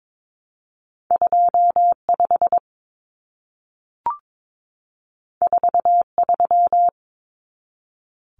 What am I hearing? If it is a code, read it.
Morse code, heard as a keyed tone, about 22 wpm, 700 Hz. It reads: 25 43
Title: Common Cut Numbers - 2-Digit 22wpm